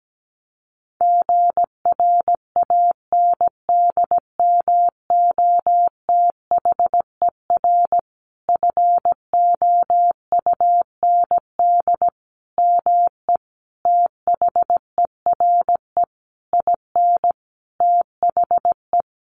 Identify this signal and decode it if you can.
Morse code, heard as a keyed tone, about 17 wpm, 700 Hz. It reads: GRANDMOTHER FOUND ME THERE IN THE
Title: Sentences from Top 1200 Words 17wpm